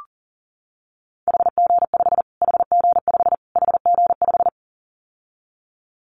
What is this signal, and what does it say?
Morse code, heard as a keyed tone, about 40 wpm, 700 Hz. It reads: HG5 HG5 HG5
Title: Callsign Prefixes for Top 25 DXCC - Repeated 3x - 40wpm